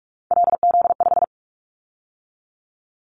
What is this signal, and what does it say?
Morse code, heard as a keyed tone, about 44 wpm, 700 Hz. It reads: LZ5
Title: Callsign Prefixes for Top 100 DXCC - 45wpm